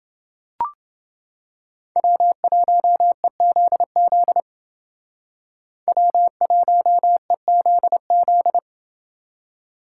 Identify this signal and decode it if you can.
Morse code, heard as a keyed tone, about 29 wpm, 700 Hz. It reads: W1EZZ W1EZZ